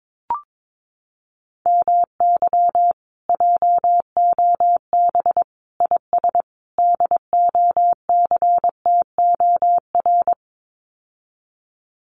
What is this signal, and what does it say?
Morse code, heard as a keyed tone, about 22 wpm, 700 Hz. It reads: MY JOB IS DOCTOR
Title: Common QSO Phrases Encore 22wpm